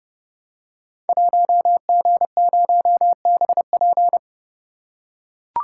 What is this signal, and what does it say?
Morse code, heard as a keyed tone, about 30 wpm, 700 Hz. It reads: 1G0BP
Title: Code Groups - 5 Letter:Numbers - 30wpm